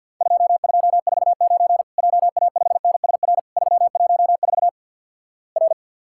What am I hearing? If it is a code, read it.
Morse code, heard as a keyed tone, about 52 wpm, 700 Hz. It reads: 2249 JA5NSR 314 R
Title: Worked All Europe DX Contest - QTC Exchange with Repeats - 50wpm